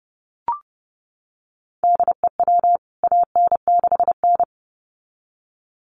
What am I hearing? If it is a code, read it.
Morse code, heard as a keyed tone, about 30 wpm, 700 Hz. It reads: DEW AN6N
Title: CWT Exchange 30wpm